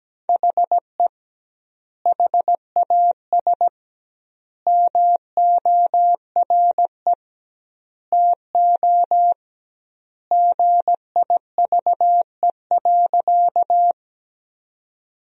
Morse code, heard as a keyed tone, about 17 wpm, 700 Hz. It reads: HE HAS MORE TO GIVE.
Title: 2x Word-Spacing - Sentences from Top 200 Words - 17wpm